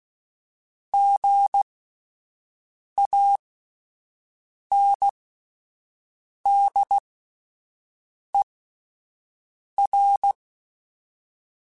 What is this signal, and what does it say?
Morse code, heard as a keyed tone, about 16 wpm, spaced as for 5 wpm, 800 Hz. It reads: GANDER